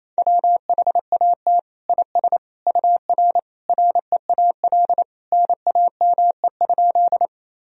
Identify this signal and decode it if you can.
Morse code, heard as a keyed tone, about 28 wpm, 700 Hz. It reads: WHAT IS UR REAL NAME?
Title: Sentences from Top 300 Words and Abbreviations 28wpm